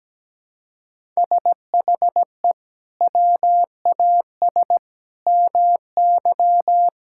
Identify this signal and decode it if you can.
Morse code, heard as a keyed tone, about 17 wpm, 700 Hz. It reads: SHE WAS MY
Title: Sentences from Top 1000 Words - Review Part 5 - 17wpm